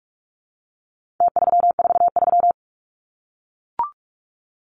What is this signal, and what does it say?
Morse code, heard as a keyed tone, about 44 wpm, 700 Hz. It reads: T343
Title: Numbers - Contest Style - 4-Digit 45wpm